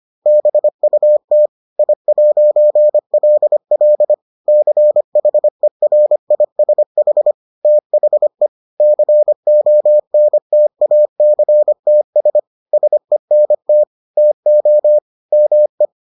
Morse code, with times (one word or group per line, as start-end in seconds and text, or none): 0.3-1.5 BUT
1.8-4.1 I'LL
4.5-7.3 CHERISH
7.7-8.5 THE
8.8-12.4 CONTACTS
12.7-13.8 SENT
14.2-15.0 TO
15.3-15.9 ME